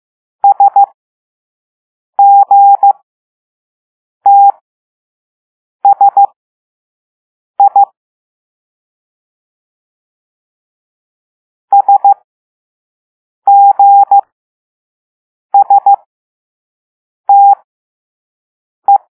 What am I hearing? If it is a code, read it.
Morse code, heard as a keyed tone, about 15 wpm, spaced as for 5 wpm, 800 Hz. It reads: SGTSI SGSTE